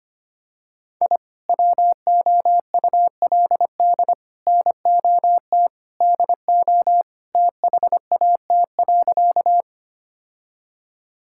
Morse code, heard as a keyed tone, about 25 wpm, 700 Hz. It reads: I WOULD NOT DO THAT.
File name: practice / 25wpm / Sentences with 3-Word Phrases - 25wpm